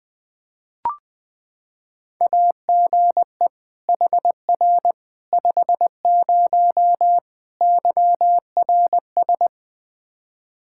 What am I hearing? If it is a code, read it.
Morse code, heard as a keyed tone, about 20 wpm, 700 Hz. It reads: AGE HR 50 YRS
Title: K1USN SST QSO Practice 20wpm